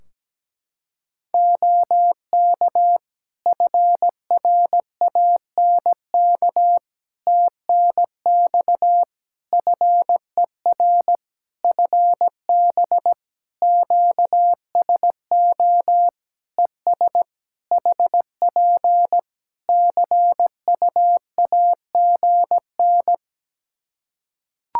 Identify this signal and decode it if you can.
Morse code, heard as a keyed tone, about 17 wpm, 700 Hz. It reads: OK FRANK TNX FER FB QSO ES HP CUAGN
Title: Practice QSO - Exchange 4 - Ending - 17wpm